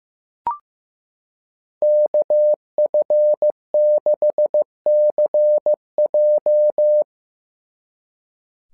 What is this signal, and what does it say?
Morse code, heard as a keyed tone, about 15 wpm, 600 Hz. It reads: KF6CJ
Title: SOTA QSO Practice 15wpm